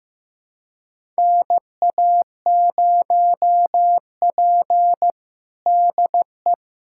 Morse code, heard as a keyed tone, about 15 wpm, 700 Hz. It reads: NA0P DE
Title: USA Callsign DE Callsign - 4 Characters - 15wpm